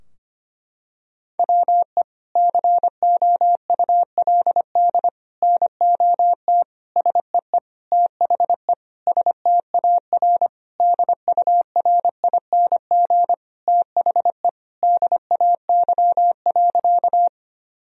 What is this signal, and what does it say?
Morse code, heard as a keyed tone, about 25 wpm, 700 Hz. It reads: WE COULD NOT SEE THE STAR DURING THE DAY.
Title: Sentences from Top 1000 Words - Review Part 2 - 25wpm